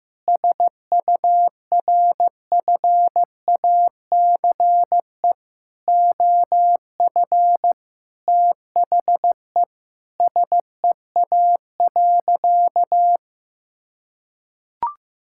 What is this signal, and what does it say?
Morse code, heard as a keyed tone, about 15 wpm, 700 Hz. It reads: SURFACE OF THE SEA.
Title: Sentences from Top 1000 Words - Review Part 5 - 15wpm